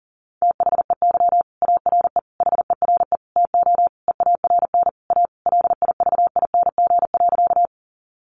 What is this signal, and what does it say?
Morse code, heard as a keyed tone, about 40 wpm, 700 Hz. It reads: THEY ARE HERE TO EARN A LIVING.
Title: Sentences from Top 1500 Words 40wpm